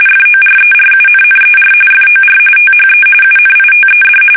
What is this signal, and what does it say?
RTTY, about 45.45 baud. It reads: FORERUNNER FOR THE UNITED
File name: fsk850 / usnavy1